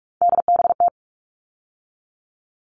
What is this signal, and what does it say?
Morse code, heard as a keyed tone, about 44 wpm, 700 Hz. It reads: DBT
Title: All Cut Numbers - 3-Digit 45wpm